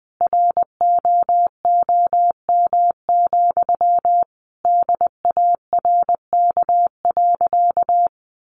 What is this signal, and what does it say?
Morse code, heard as a keyed tone, about 20 wpm, 700 Hz. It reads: ROOM, DARK.
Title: Sentences from Top 1400 Words 20wpm